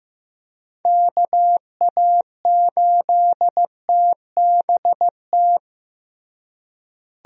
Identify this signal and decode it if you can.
Morse code, heard as a keyed tone, about 15 wpm, 700 Hz. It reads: KA8TBT
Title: USA Call Signs - 6 Characters - Repeated 2x - 15wpm